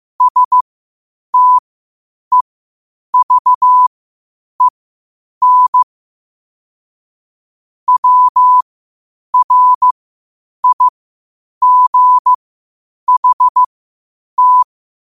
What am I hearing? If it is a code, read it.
Morse code, heard as a keyed tone, about 15 wpm, spaced as for 8 wpm, 1000 Hz. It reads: STEVEN WRIGHT
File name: qotdfiles / 8wpm